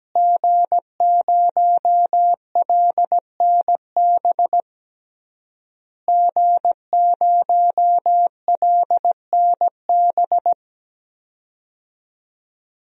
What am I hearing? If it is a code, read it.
Morse code, heard as a keyed tone, about 17 wpm, 700 Hz. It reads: G0LNB G0LNB
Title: DXCC-100 Call Signs - 5 Characters - Repeated 3x - 17wpm